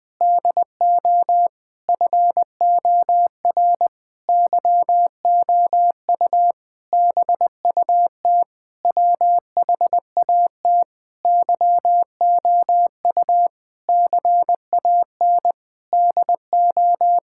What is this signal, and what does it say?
Morse code, heard as a keyed tone, about 20 wpm, 700 Hz. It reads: DO FOR YOU BUT WHAT YOU CAN DO
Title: Sentences with 9-Word Phrases - 20wpm